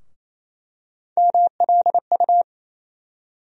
Morse code, heard as a keyed tone, about 27 wpm, 700 Hz. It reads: MLU
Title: USA Call Sign Postfixes - 3 Letters 28wpm